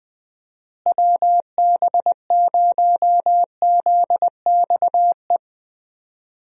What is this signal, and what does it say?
Morse code, heard as a keyed tone, about 20 wpm, 700 Hz. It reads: WB0ZXE
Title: USA Call Signs - 6 Characters - Repeated 3x - 20wpm